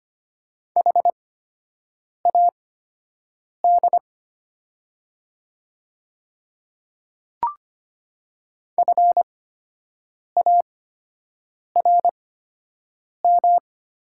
Morse code, heard as a keyed tone, about 25 wpm, spaced as for 7 wpm, 700 Hz. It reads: HAD FARM
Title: 8x Character Spacing - Top 500 Words - 25wpm